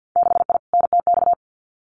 Morse code, heard as a keyed tone, about 50 wpm, 700 Hz. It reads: 6I NTX
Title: Winter Field Day Exchanges Encore 50wpm